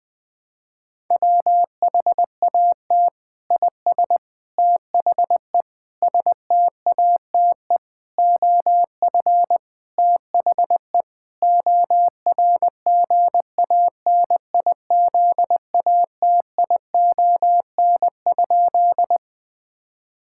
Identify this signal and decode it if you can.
Morse code, heard as a keyed tone, about 20 wpm, 700 Hz. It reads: WHAT IS THE STATE OF THE ORGANIZATION?
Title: Sentences from Top 1000 Words - Review Part 5 - 20wpm